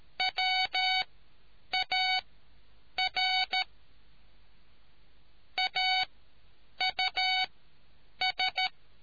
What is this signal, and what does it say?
Morse code, heard as a keyed tone, about 13 wpm, spaced as for 8 wpm, 750 Hz. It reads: WAR AUS